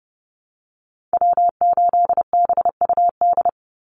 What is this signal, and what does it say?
Morse code, heard as a keyed tone, about 30 wpm, 700 Hz. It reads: W8BUD
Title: Call Signs Repeated 2x - 30wpm